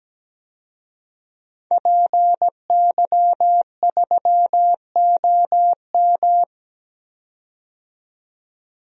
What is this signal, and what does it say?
Morse code, heard as a keyed tone, about 17 wpm, 700 Hz. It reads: PY3OM
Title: DXCC-100 Call Signs - 5 Characters - 17wpm